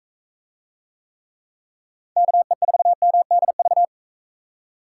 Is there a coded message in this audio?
Yes: Morse code from 2.2 to 3.9 s, heard as a keyed tone, about 40 wpm, 700 Hz. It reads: KE4MDV